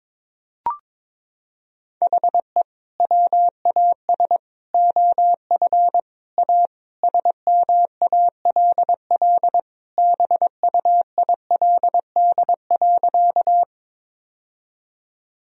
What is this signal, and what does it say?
Morse code, heard as a keyed tone, about 22 wpm, 700 Hz. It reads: HE WAS OF A SMALL BUILD.
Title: Sentences from Top 1000 Words - Review Part 1 - 22wpm